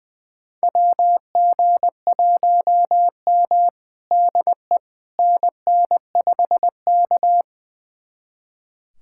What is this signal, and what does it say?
Morse code, heard as a keyed tone, about 20 wpm, 700 Hz. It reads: WG1M DE NN5K
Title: USA Callsign DE Callsign - 4 Characters - 20wpm